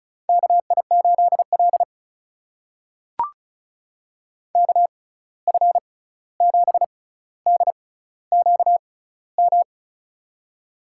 Morse code, heard as a keyed tone, about 36 wpm, spaced as for 16 wpm, 700 Hz. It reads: KI8L KF7DQM
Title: USA Call Signs - 6x Character Spacing - 35wpm